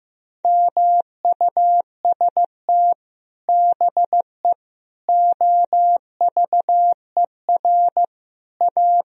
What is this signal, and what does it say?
Morse code, heard as a keyed tone, about 15 wpm, 700 Hz. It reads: MUST BE OVER A
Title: Sentences from Top 1000 Words - Review Part 2 - 15wpm